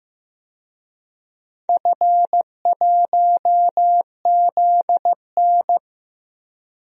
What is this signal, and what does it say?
Morse code, heard as a keyed tone, about 15 wpm, 700 Hz. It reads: F1ZN
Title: DXCC-100 Call Signs - 4 Characters - 15wpm